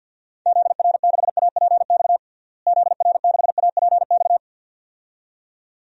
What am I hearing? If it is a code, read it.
Morse code, heard as a keyed tone, about 50 wpm, 700 Hz. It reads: ZR6APX ZR6APX
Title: DXCC-100 Call Signs - 6 Characters - Repeated 2x - 50wpm